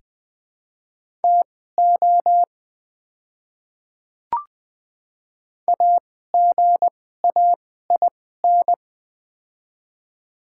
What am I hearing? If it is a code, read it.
Morse code, heard as a keyed tone, about 20 wpm, spaced as for 15 wpm, 700 Hz. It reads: TO AGAIN
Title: 2x Character Spacing - Top 300 Words - 20wpm